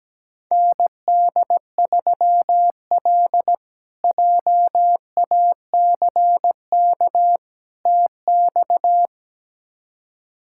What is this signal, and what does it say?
Morse code, heard as a keyed tone, about 17 wpm, 700 Hz. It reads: ND3L JACK TX
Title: K1USN SST QSO Practice - 17wpm